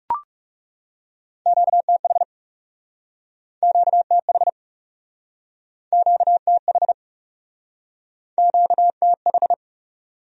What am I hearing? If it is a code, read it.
Morse code, heard as a keyed tone, about 40 wpm, 700 Hz. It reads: QTH QTH QTH QTH